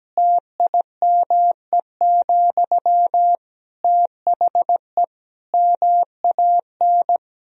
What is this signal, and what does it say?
Morse code, heard as a keyed tone, about 17 wpm, 700 Hz. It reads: TIME, THE MAN
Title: Sentences from Top 1000 Words - Review Part 2 - 17wpm